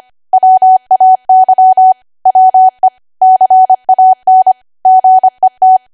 Morse code, heard as a keyed tone, about 25 wpm, 750 Hz. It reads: WAY WE CAN GET
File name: Archive / 25 WPM / 25WPM